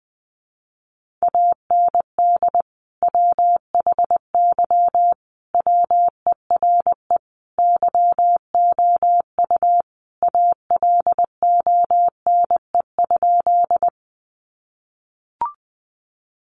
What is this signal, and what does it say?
Morse code, heard as a keyed tone, about 20 wpm, 700 Hz. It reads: AND WHY WERE YOU ALONE?
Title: Sentences from Top 1000 Words 20wpm